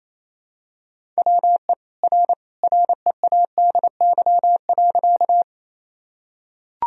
Morse code, heard as a keyed tone, about 27 wpm, 700 Hz. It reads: WE R READY.
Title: Sentences from Top 400 Words and Abbreviations 28wpm